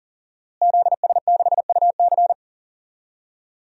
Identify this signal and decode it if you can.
Morse code, heard as a keyed tone, about 37 wpm, 700 Hz. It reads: ZS6UC
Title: DXCC-100 Call Signs - 5 Characters - Repeated 2x - 40wpm